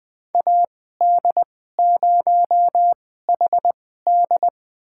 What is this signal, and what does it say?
Morse code, heard as a keyed tone, about 20 wpm, spaced as for 14 wpm, 700 Hz. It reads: AD0HD
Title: USA Call Signs - 5 Characters - 2x Character Spacing - 20wpm